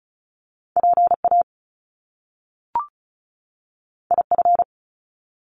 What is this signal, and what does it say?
Morse code, heard as a keyed tone, about 36 wpm, 700 Hz. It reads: PA IF
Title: Top Two Letter Combinations in Words 35wpm